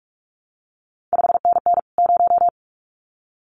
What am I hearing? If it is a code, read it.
Morse code, heard as a keyed tone, about 46 wpm, 700 Hz. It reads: 5NN 0
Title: CQ WW DX Contest Exchange Encore - 45wpm